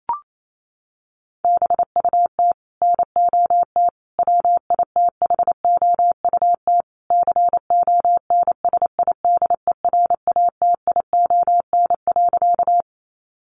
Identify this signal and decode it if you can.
Morse code, heard as a keyed tone, about 28 wpm, 700 Hz. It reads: BUT NOT WITHOUT CONSIDERATION.
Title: Sentences from Top 1500 Words 28wpm